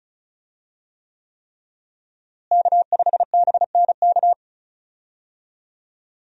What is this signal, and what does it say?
Morse code, heard as a keyed tone, about 36 wpm, 700 Hz. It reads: K5BNK
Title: USA Call Signs - 5 Characters - Repeated 3x - 35wpm